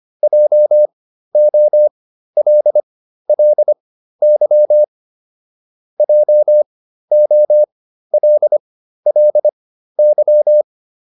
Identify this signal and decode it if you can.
Morse code, heard as a keyed tone, about 25 wpm, spaced as for 13 wpm, 600 Hz. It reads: JOLLY JOLLY
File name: LY5 13